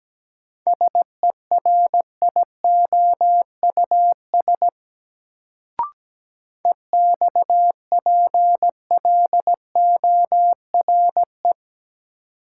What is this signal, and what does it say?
Morse code, heard as a keyed tone, about 17 wpm, 700 Hz. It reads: SERIOUS EXPLORE
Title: Sentences from Top 1100 Words - New Word Drill Alt - 17wpm